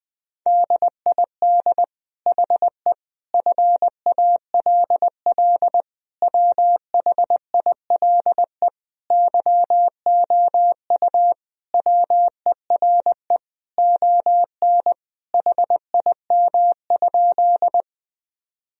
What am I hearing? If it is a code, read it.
Morse code, heard as a keyed tone, about 20 wpm, 700 Hz. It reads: DID HE FALL WHILE YOU WERE ON HIM?
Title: Sentences from Top 1000 Words - Review Part 2 - 20wpm